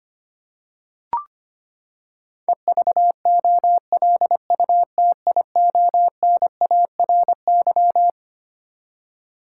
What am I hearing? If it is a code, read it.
Morse code, heard as a keyed tone, about 25 wpm, 700 Hz. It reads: EVOLUTIONARY
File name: Twelve Letter Words - 25wpm